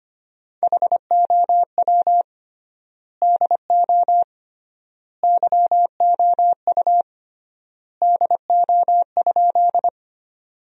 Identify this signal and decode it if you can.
Morse code, heard as a keyed tone, about 25 wpm, 700 Hz. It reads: HOW DO YOU DO?
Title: 3x Word-Spacing - Sentences from Top 100 Words - 25wpm